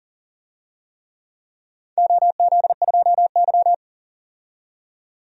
Morse code, heard as a keyed tone, about 40 wpm, 700 Hz. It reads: OZ2Y
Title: DXCC-100 Call Signs - 4 Characters - 40wpm